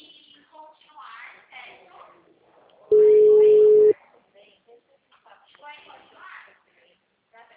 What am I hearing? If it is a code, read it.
Ringback tone (Europe, 425 Hz), with speech.
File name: out- -2015- - - .441